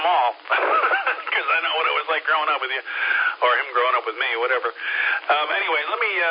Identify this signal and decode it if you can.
Emergency Alert System (SAME) tones (header bursts), heard as harsh buzzing bursts, with speech. It is heard from 0.9 to 5.2 s.